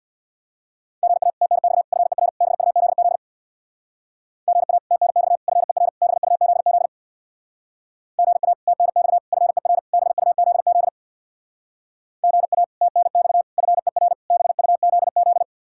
Morse code, heard as a keyed tone, about 66 wpm, 700 Hz. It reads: GA TNX FER BUZZ GA TNX FER BUZZ GA TNX FER BUZZ GA TNX FER BUZZ